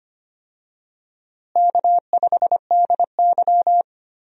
Morse code, heard as a keyed tone, about 25 wpm, 700 Hz. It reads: K5DY
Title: USA Call Signs - 4 Characters - Repeated 3x - 25wpm